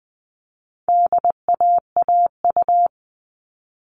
Morse code, heard as a keyed tone, about 20 wpm, 700 Hz. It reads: DAAU
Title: All Cut Numbers - 4-Digit 20wpm